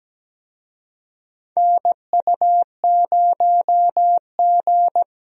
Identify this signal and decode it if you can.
Morse code, heard as a keyed tone, about 17 wpm, 700 Hz. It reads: NU0G
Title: USA Call Signs - 4 Characters - Repeated 2x - 17wpm